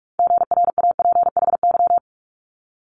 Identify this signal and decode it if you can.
Morse code, heard as a keyed tone, about 46 wpm, 700 Hz. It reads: GRAPHY
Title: Multi-letter Phonemes Quick Fire 45wpm